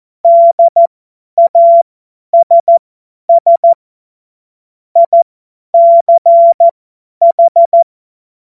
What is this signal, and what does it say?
Morse code, heard as a keyed tone, about 14 wpm, spaced as for 10 wpm, 680 Hz. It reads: DASS ICH
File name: Morsekurs / Witze / Witz 02 Januar 2024 50 F70